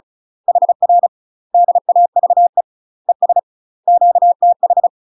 Morse code, heard as a keyed tone, about 35 wpm, 700 Hz. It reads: HR DAVE ES QTH
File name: UK MASTER QSOs TRAINING/35WPM